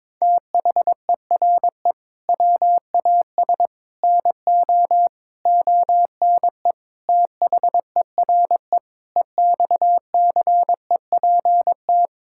Morse code, heard as a keyed tone, about 22 wpm, 700 Hz. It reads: THERE WAS NO ONE THERE EXCEPT
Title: Sentences from Top 1600 Words 22wpm